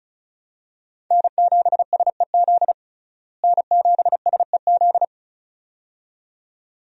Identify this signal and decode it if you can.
Morse code, heard as a keyed tone, about 36 wpm, 700 Hz. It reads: N7SEZ N7SEZ
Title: USA Call Signs - 5 Characters - Repeated 2x - 35wpm